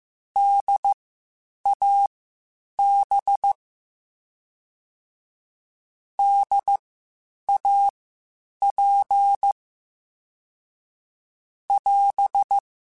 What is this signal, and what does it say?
Morse code, heard as a keyed tone, about 15 wpm, spaced as for 8 wpm, 800 Hz. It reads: DAB DAP <AS>
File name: Spaced Alphas / Spaced Alphas mp3s / Ds sp @13